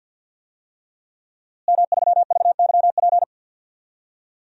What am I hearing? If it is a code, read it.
Morse code, heard as a keyed tone, about 50 wpm, 700 Hz. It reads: M3VYP
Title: DXCC-100 Call Signs - 5 Characters - Repeated 3x - 50wpm